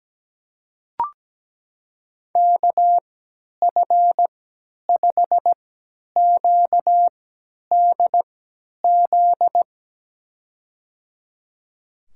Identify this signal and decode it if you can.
Morse code, heard as a keyed tone, about 17 wpm, spaced as for 10 wpm, 700 Hz. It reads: KF5QDZ
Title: USA Call Signs - 3x Character Spacing - 17wpm